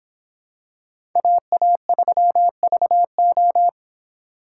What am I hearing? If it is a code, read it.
Morse code, heard as a keyed tone, about 26 wpm, 700 Hz. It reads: AA3VO